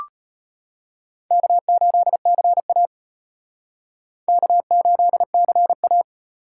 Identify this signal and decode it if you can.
Morse code, heard as a keyed tone, about 31 wpm, 700 Hz. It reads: K8CA K8CA